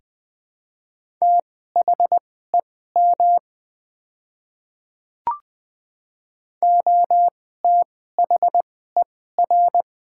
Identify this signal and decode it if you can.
Morse code, heard as a keyed tone, about 20 wpm, spaced as for 15 wpm, 700 Hz. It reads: THEM OTHER
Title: 2x Character Spacing - Top 100 Words - 20wpm